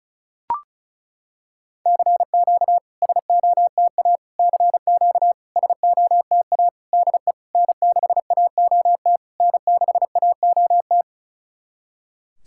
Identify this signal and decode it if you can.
Morse code, heard as a keyed tone, about 36 wpm, 700 Hz. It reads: CQ SOTA CQ SOTA DE N6AOT N6AOT
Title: SOTA QSO Practice 35wpm